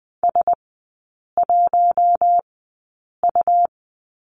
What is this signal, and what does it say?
Morse code, heard as a keyed tone, about 20 wpm, spaced as for 8 wpm, 700 Hz. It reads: S1U
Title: 100 Lesson 17e - Letter u - Three Characters 20wpm